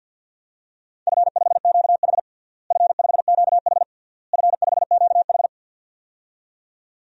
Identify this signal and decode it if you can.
Morse code, heard as a keyed tone, about 51 wpm, 700 Hz. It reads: F5QH F5QH F5QH
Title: DXCC-100 Call Signs - 4 Characters - Repeated 3x - 50wpm